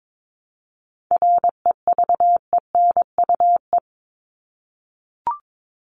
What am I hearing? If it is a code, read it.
Morse code, heard as a keyed tone, about 22 wpm, 700 Hz. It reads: REVENUE
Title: Top 1501-1750 Words 22wpm